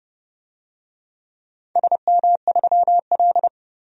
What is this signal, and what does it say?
Morse code, heard as a keyed tone, about 30 wpm, 700 Hz. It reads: SM3L
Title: DXCC-100 Call Signs - 4 Characters - 30wpm